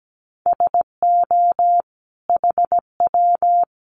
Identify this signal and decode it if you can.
Morse code, heard as a keyed tone, about 17 wpm, 700 Hz. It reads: SO HW
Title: Common QSO Phrases 17wpm